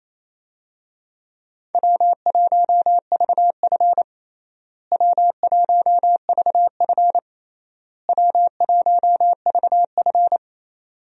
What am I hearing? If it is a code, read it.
Morse code, heard as a keyed tone, about 28 wpm, 700 Hz. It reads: W1VF W1VF W1VF
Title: USA Call Signs - 4 Characters - Repeated 3x - 28wpm